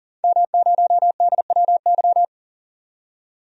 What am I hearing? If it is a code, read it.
Morse code, heard as a keyed tone, about 38 wpm, 700 Hz. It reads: M0DWY
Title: DXCC-100 Call Signs - 5 Characters - 40wpm